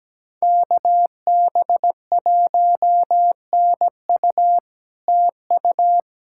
Morse code, heard as a keyed tone, about 17 wpm, 700 Hz. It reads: KB1NU TU
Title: SOTA QSO Practice 17wpm